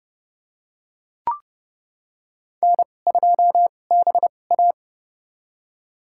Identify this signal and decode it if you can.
Morse code, heard as a keyed tone, about 31 wpm, spaced as for 22 wpm, 700 Hz. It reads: N2BA
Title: USA Call Signs - 4 Characters - 2x Character Spacing - 30wpm